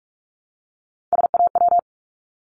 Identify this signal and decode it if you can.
Morse code, heard as a keyed tone, about 46 wpm, 700 Hz. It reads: SAW
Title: Multi-letter Phonemes Repeated 3x 45wpm